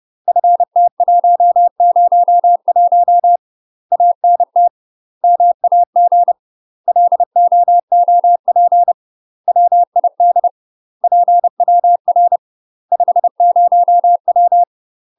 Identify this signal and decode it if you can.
Morse code, heard as a keyed tone, about 30 wpm, 700 Hz. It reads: FT101 ANT MAG LOOP WID PWR 50W
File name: UK MASTER QSOs TRAINING/30WPM